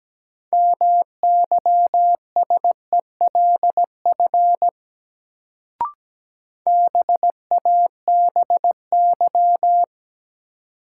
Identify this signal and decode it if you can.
Morse code, heard as a keyed tone, about 17 wpm, 700 Hz. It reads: MYSELF BABY
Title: Sentences from Top 1300 Words - New Word Drill - 17wpm